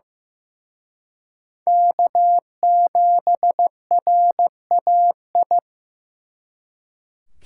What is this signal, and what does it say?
Morse code, heard as a keyed tone, about 18 wpm, 700 Hz. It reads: K7RAI
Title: Speed Racing - USA 1x3 Call Signs 15wpm